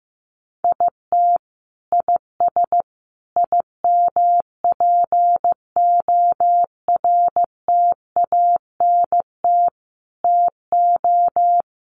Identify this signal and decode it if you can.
Morse code, heard as a keyed tone, about 15 wpm, 700 Hz. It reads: IT IS IMPORTANT TO
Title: Sets of 5 Words 15wpm